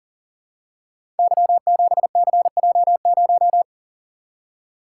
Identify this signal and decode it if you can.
Morse code, heard as a keyed tone, about 39 wpm, 700 Hz. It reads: Y7CJ0
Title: Code Groups - 5 Letter:Numbers - 40wpm